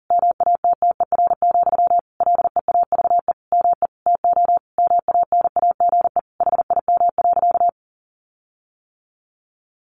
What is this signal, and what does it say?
Morse code, heard as a keyed tone, about 40 wpm, 700 Hz. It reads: MATTER, LEAVE ME TO MANAGE HIM.
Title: Sentences from Top 1400 Words 40wpm